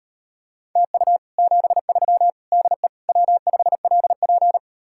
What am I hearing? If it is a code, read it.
Morse code, heard as a keyed tone, about 38 wpm, 700 Hz. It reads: TU 73 DE W5LP